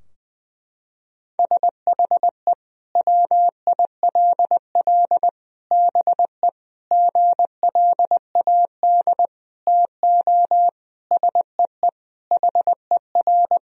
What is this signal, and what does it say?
Morse code, heard as a keyed tone, about 20 wpm, 700 Hz. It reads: SHE WILL BE GLAD TO SEE HER
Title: Sentences from Top 1000 Words - Review Part 2 - 20wpm